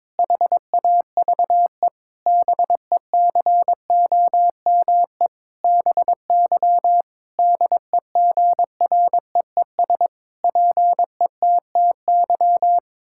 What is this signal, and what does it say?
Morse code, heard as a keyed tone, about 22 wpm, 700 Hz. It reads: HAVE BECOME BY DEGREES PETTY
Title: Pangram Challenge v2 - 22wpm